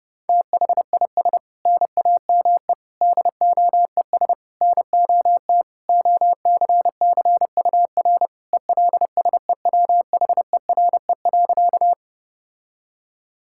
Morse code, heard as a keyed tone, about 30 wpm, 700 Hz. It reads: THIS NAME DOES NOT OCCUR ELSEWHERE.
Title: Sentences from Top 1600 Words 30wpm